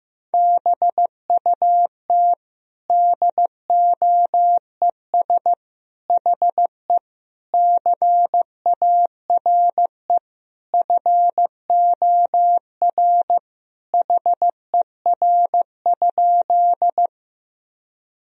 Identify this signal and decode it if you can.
Morse code, heard as a keyed tone, about 16 wpm, 700 Hz. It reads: BUT DOES HE CARE FOR HER?
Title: Sentences from Top 1000 Words - Review Part 1 - 15wpm